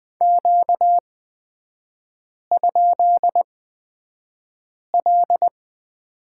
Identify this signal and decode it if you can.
Morse code, heard as a keyed tone, about 20 wpm, spaced as for 5 wpm, 700 Hz. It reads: Q?L
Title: 236 Lesson 38e - Letter q - Three Characters 20wpm